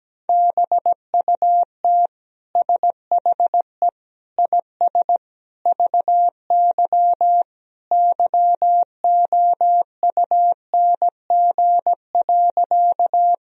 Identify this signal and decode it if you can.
Morse code, heard as a keyed tone, about 17 wpm, 700 Hz. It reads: BUT SHE IS VY YOUNG.
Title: Sentences from Top 400 Words and Abbreviations 17wpm